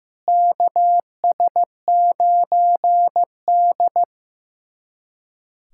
Morse code, heard as a keyed tone, about 15 wpm, 700 Hz. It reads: KS9D
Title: USA Callsign DE Callsign - 4 Characters - 15wpm